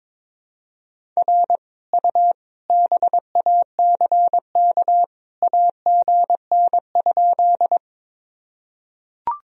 Morse code, heard as a keyed tone, about 22 wpm, 700 Hz. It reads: R U BACK AGN?
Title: Sentences from Top 200 Words and Abbreviations 22wpm